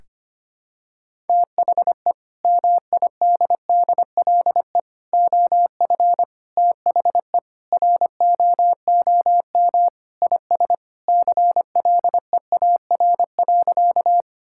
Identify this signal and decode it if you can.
Morse code, heard as a keyed tone, about 25 wpm, 700 Hz. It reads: THE MIDDLE OF THE ROOM IS CLEAR.
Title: Sentences from Top 1000 Words - Review Part 6 - 25wpm